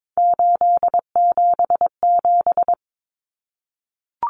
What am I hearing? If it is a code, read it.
Morse code, heard as a keyed tone, about 22 wpm, 700 Hz. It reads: 877
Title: Common Cut Numbers - 3-Digit 22wpm